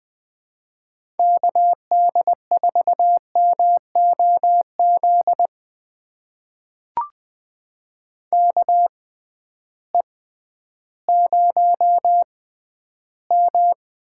Morse code, heard as a keyed tone, about 20 wpm, spaced as for 12 wpm, 700 Hz. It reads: KD4MOZ KE0M
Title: USA Call Signs - 6 Characters - 6x Character Spacing - 20wpm